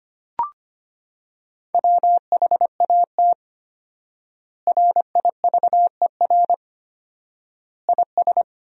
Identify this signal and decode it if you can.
Morse code, heard as a keyed tone, about 25 wpm, 700 Hz. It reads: WHAT RIVER IS
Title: 4x Word-Spacing - Sentences from Top 300 Words - 25wpm